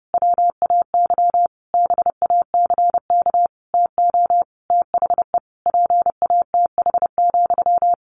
Morse code, heard as a keyed tone, about 30 wpm, 700 Hz. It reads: WAY BACK TO THE PATH,
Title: A Sound Of Thunder 30wpm